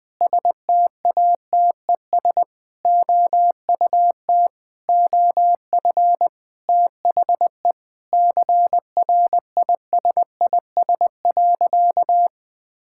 Morse code, heard as a keyed tone, about 20 wpm, 700 Hz. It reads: STATES OUT OF THE CRISIS.
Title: Sentences with 8-Word Phrases - Spoken Beginning - 20wpm